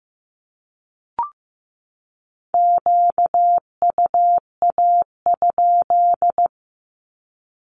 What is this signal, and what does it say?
Morse code, heard as a keyed tone, about 15 wpm, 700 Hz. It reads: QUA?
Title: Q-Codes 15wpm